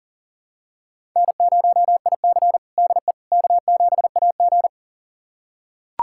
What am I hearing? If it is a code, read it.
Morse code, heard as a keyed tone, about 40 wpm, 700 Hz. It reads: N0IC DE K7AG
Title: USA Callsign DE Callsign - 4 Characters - 40wpm